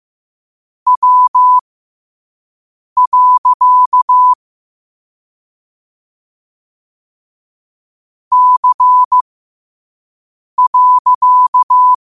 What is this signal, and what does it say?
Morse code, heard as a keyed tone, about 15 wpm, spaced as for 5 wpm, 1000 Hz. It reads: W. C.
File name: qotdfiles / 5wpm